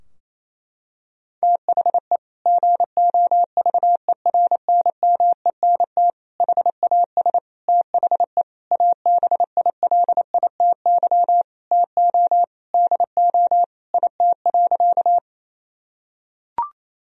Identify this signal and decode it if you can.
Morse code, heard as a keyed tone, about 28 wpm, 700 Hz. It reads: THE GOVERNMENT HAS THE ABILITY TO DO IT.
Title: Sentences from Top 1000 Words - Review Part 3 - 28wpm